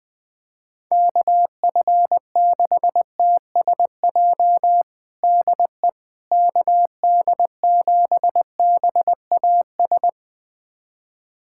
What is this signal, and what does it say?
Morse code, heard as a keyed tone, about 21 wpm, 700 Hz. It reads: KF6TSJ DE KD7BAS
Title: USA Callsign DE Callsign - 6 Characters - 20wpm